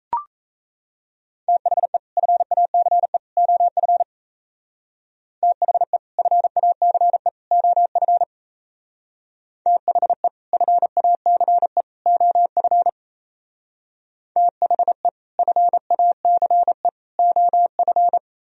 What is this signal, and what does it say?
Morse code, heard as a keyed tone, about 36 wpm, 700 Hz. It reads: THE FACE OF THE FACE OF THE FACE OF THE FACE OF